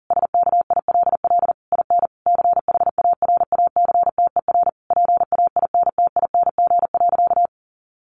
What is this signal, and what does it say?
Morse code, heard as a keyed tone, about 40 wpm, 700 Hz. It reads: SKILL IN CHARACTER PAINTING.
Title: Sentences from Top 1200 Words 40wpm